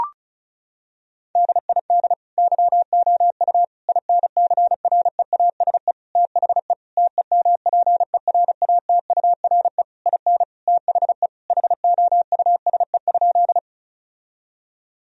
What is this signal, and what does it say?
Morse code, heard as a keyed tone, about 36 wpm, 700 Hz. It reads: DID YOU INCREASE THE TEMPERATURE IN THE HOUSE?
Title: Sentences from Top 1000 Words - Review Part 4 - 35wpm